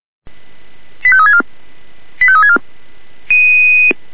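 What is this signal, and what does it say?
Five-tone selective call (ZVEI), heard as quick runs of five tones, tones of about 70 ms. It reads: 85364 85364
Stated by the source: ZVEI five-tone call received by SDR.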